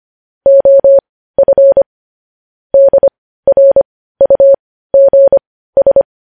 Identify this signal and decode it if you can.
Morse code, heard as a keyed tone, about 25 wpm, spaced as for 15 wpm, 550 Hz. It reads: OF DRUGS
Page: news headlines in morse code at 15 wpm